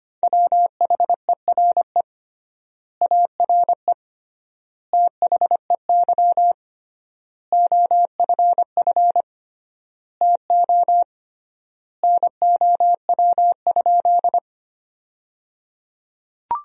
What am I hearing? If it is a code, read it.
Morse code, heard as a keyed tone, about 25 wpm, 700 Hz. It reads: WHERE ARE THEY OFF TO NOW?
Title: 3x Word-Spacing - Sentences from Top 200 Words - 25wpm